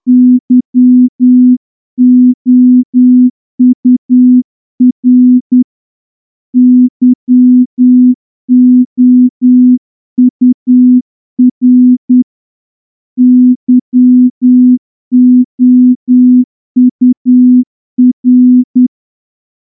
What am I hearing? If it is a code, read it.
Morse code, heard as a keyed tone, about 10 wpm, 250 Hz. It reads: YOUR YOUR YOUR
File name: morse-training-10WPM